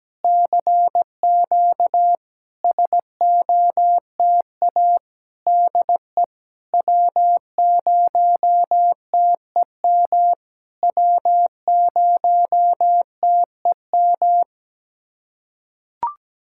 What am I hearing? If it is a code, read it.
Morse code, heard as a keyed tone, about 17 wpm, 700 Hz. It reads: CQ SOTA DE W0TEM W0TEM
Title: SOTA QSO Practice 17wpm